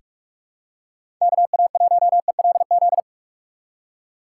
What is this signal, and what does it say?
Morse code, heard as a keyed tone, about 46 wpm, 700 Hz. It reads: KA1ELZ
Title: USA Call Signs - 6 Characters - 3x Character Spacing - 45wpm